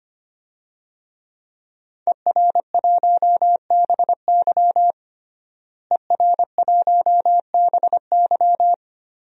Morse code, heard as a keyed tone, about 25 wpm, 700 Hz. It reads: ER1BY ER1BY
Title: DXCC-100 Call Signs - 5 Characters - Repeated 2x - 25wpm